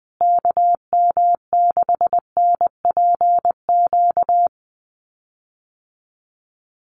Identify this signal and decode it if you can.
Morse code, heard as a keyed tone, about 20 wpm, 700 Hz. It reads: KM6NPQ
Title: USA Call Signs Repeated 3x 20wpm